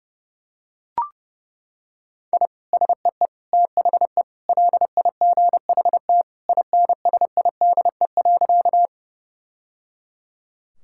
Morse code, heard as a keyed tone, about 30 wpm, 700 Hz. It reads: I SEE THE LIGHT INSIDE.
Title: Sentences from Top 1000 Words - Review Part 6 - 30wpm